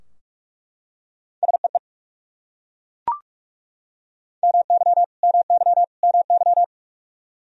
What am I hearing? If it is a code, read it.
Morse code, heard as a keyed tone, about 42 wpm, 700 Hz. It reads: SEE MY MY MY
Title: Top 100 Words Repeated 45wpm